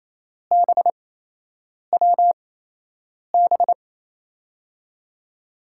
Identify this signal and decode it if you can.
Morse code, heard as a keyed tone, about 28 wpm, spaced as for 8 wpm, 700 Hz. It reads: BWB
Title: USA Call Signs - 5 Characters - 8x Character Spacing - 28wpm